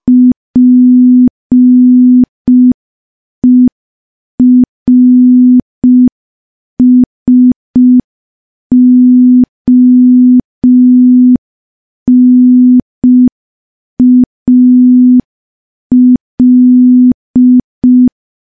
Morse code, heard as a keyed tone, about 5 wpm, 250 Hz. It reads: PERSONAL
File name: morse / morse-training-05WPM